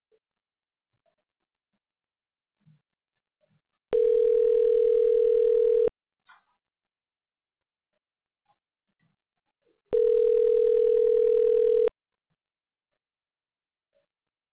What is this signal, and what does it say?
Ringback tone (North America), 440 + 480 Hz.
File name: recordings / RE42dc55933d7bb9274aee381e99e7f39f